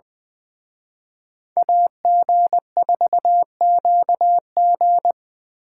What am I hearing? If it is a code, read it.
Morse code, heard as a keyed tone, about 24 wpm, 700 Hz. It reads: AG4QG
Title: Speed Racing - USA 2x2 Call Signs 20wpm